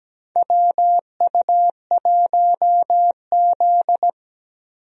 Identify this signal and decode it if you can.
Morse code, heard as a keyed tone, about 17 wpm, 700 Hz. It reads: WU1Z
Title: USA Call Signs - 4 Characters - Repeated 3x - 17wpm